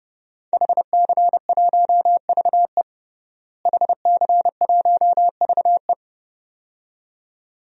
Morse code, heard as a keyed tone, about 30 wpm, 700 Hz. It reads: HC1VE HC1VE
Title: DXCC-100 Call Signs - 5 Characters - Repeated 2x - 30wpm